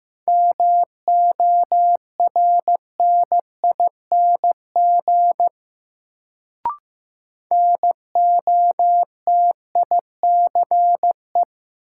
Morse code, heard as a keyed tone, about 15 wpm, 700 Hz. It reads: MORNING NOTICE
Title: Sentences from Top 400 Words - New Word Drill - 15wpm